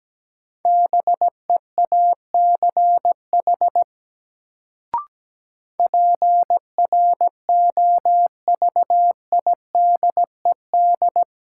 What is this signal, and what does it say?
Morse code, heard as a keyed tone, about 17 wpm, 700 Hz. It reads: BEACH PROVIDED
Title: Sentences from Top 1500 Words - New Word Drill Alt - 17wpm